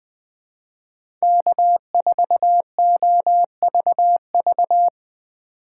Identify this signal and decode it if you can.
Morse code, heard as a keyed tone, about 20 wpm, 700 Hz. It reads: K4OVV